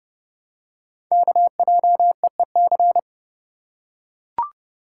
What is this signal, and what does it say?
Morse code, heard as a keyed tone, about 30 wpm, 700 Hz. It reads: KJEEC
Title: Code Groups - 5 Letters - 30wpm